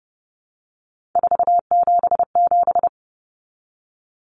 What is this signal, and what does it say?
Morse code, heard as a keyed tone, about 30 wpm, 700 Hz. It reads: 477
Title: Common Cut Numbers - 3-Digit 30wpm